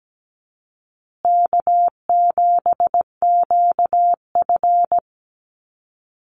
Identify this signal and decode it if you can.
Morse code, heard as a keyed tone, about 17 wpm, 700 Hz. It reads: K7QF
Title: USA Call Signs Repeated 2x 17wpm